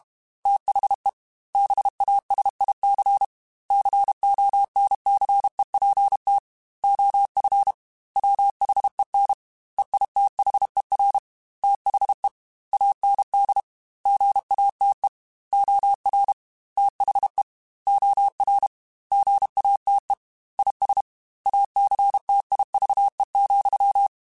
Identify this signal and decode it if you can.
Morse code, heard as a keyed tone, about 32 wpm, 800 Hz. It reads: THE BASIC CONCEPT OF WHEN EITHER THE AND GATE OR THE OR GATE IS ACTIVE,
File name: Technical Subjects / Technical Subjects-mp3s / AND Gate @30